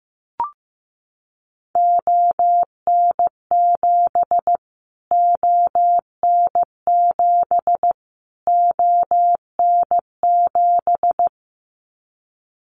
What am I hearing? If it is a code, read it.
Morse code, heard as a keyed tone, about 15 wpm, 700 Hz. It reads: ON7 ON7 ON7
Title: Callsign Prefixes for Top 25 DXCC - Repeated 3x - 15wpm